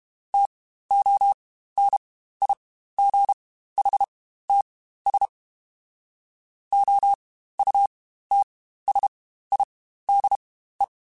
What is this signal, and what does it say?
Morse code, heard as a keyed tone, about 32 wpm, spaced as for 15 wpm, 800 Hz. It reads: TONIGHTS OUTSIDE
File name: Spaced Characters / Spaced Characters-mp3s / Weather 3 17 sp @30